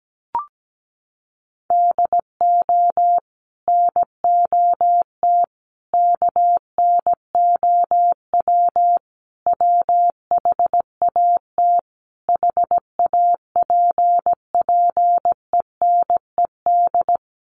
Morse code, heard as a keyed tone, about 17 wpm, 700 Hz. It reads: DO NOT KNOW WHAT HAPPENED
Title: Sets of 5 Words Encore2 17wpm